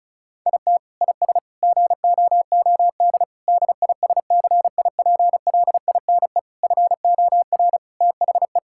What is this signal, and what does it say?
Morse code, heard as a keyed tone, about 36 wpm, 700 Hz. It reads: IT IS GOOD DISCIPLINE FOR THE
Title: Sentences from Top 1000 Words - Review Part 6 - 35wpm